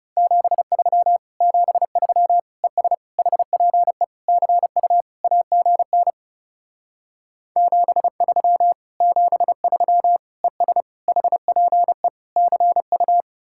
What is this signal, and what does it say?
Morse code, heard as a keyed tone, about 38 wpm, 700 Hz. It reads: 73 73 ES HPE CU AGN 73 73 ES HPE CU
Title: Speed-Racing Common QSO Phrases Encore - 30wpm